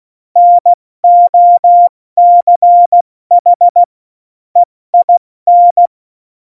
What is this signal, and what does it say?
Morse code, heard as a keyed tone, about 16 wpm, 700 Hz. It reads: NOCH EIN
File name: Morsekurs / Witze / Witz 20. Oktober 2020, 70 F80